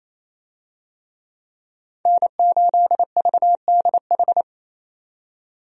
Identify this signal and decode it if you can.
Morse code, heard as a keyed tone, about 28 wpm, 700 Hz. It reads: N8VDH
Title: USA Call Signs - 5 Characters - 28wpm